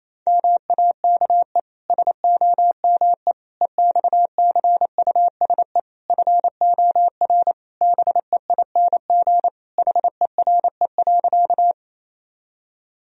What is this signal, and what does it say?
Morse code, heard as a keyed tone, about 28 wpm, 700 Hz. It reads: MAKE SOME EXCUSE FOR BEING HERE.
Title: Sentences from Top 1000 Words - Review Part 3 - 28wpm